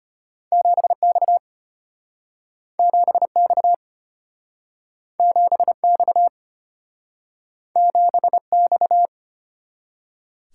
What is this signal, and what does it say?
Morse code, heard as a keyed tone, about 34 wpm, 700 Hz. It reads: 7X 7X 7X 7X